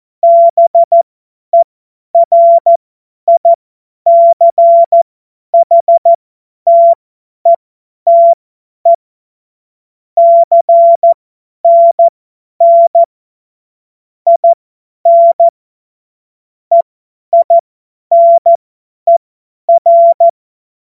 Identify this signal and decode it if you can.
Morse code, heard as a keyed tone, about 14 wpm, spaced as for 10 wpm, 680 Hz. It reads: BERICHTETE CNN IN EINER
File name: Morsekurs / Witze / Witz 09. November 2021, 50 F70